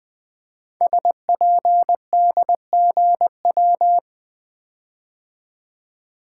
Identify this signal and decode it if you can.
Morse code, heard as a keyed tone, about 20 wpm, 700 Hz. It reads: SPDGW
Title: Code Groups - 5 Letters - 20wpm